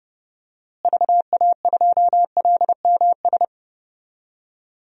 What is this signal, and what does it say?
Morse code, heard as a keyed tone, about 32 wpm, 700 Hz. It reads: VA2LMS
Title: DXCC-100 Call Signs - 6 Characters - Repeated 2x - 30wpm